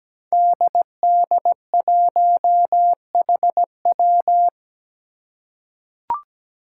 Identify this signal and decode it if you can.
Morse code, heard as a keyed tone, about 17 wpm, 700 Hz. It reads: DD1HW
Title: Five Challenging Letter-Number Combinations 17wpm